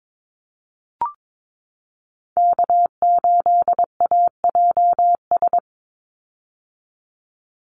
Morse code, heard as a keyed tone, about 22 wpm, 700 Hz. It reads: K8AJS
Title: Call Signs - 22wpm